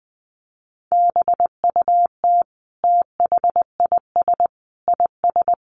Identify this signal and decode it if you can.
Morse code, heard as a keyed tone, about 19 wpm, 700 Hz. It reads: BUT THIS IS
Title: Sentences from Top 1100 Words 20wpm